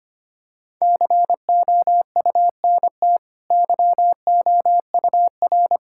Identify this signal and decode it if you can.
Morse code, heard as a keyed tone, about 25 wpm, 700 Hz. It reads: COUNT YOUR
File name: Easy Sentences with Hints - 25wpm